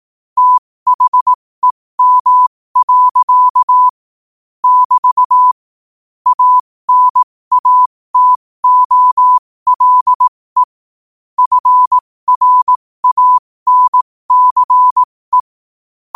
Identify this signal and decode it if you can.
Morse code, heard as a keyed tone, about 18 wpm, spaced as for 15 wpm, 1000 Hz. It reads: THEM. <BT> ANATOLE FRANCE
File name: qotdfiles / 15wpm